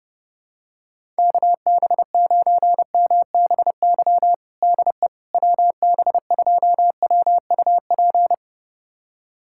Morse code, heard as a keyed tone, about 32 wpm, 700 Hz. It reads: KB9MBY DE WB2WUP
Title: USA Callsign DE Callsign - 6 Characters - 30wpm